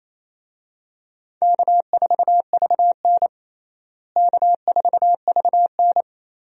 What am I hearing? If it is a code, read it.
Morse code, heard as a keyed tone, about 28 wpm, 700 Hz. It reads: K4VN K4VN
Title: USA Call Signs - 4 Characters - Repeated 2x - 28wpm